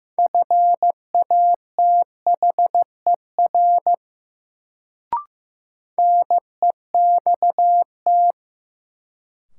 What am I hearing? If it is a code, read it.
Morse code, heard as a keyed tone, about 15 wpm, 700 Hz. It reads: FATHER NEXT
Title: Sentences from Top 300 Words - New Word Drill - 15wpm